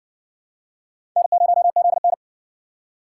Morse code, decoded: N0ZN

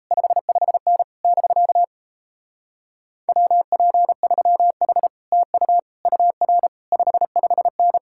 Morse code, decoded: 55N <BK> WP3H TU UR 55N

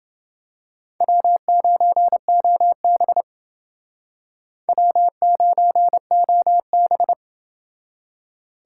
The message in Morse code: W9OB W9OB

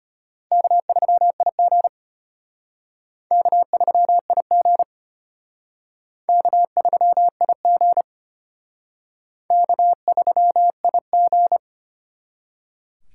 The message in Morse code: K3IG K3IG K3IG K3IG

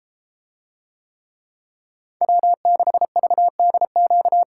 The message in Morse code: W6VDQ

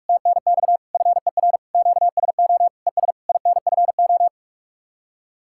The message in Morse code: TNX FER QSO ES INFO

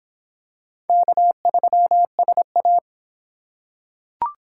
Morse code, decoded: K3SA

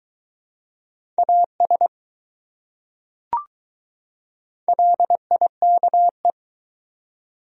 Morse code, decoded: AS LIKE

